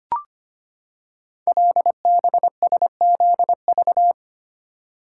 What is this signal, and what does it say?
Morse code, heard as a keyed tone, about 25 wpm, 700 Hz. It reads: LBSZV
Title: Code Groups - 5 Letters - 25wpm